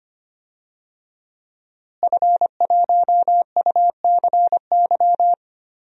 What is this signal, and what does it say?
Morse code, heard as a keyed tone, about 25 wpm, 700 Hz. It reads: F1UCY
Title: DXCC-100 Call Signs - 5 Characters - Repeated 3x - 25wpm